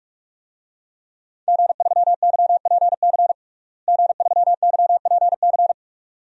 Morse code, heard as a keyed tone, about 41 wpm, 700 Hz. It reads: G3YPC G3YPC
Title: DXCC-100 Call Signs - 5 Characters - Repeated 2x - 45wpm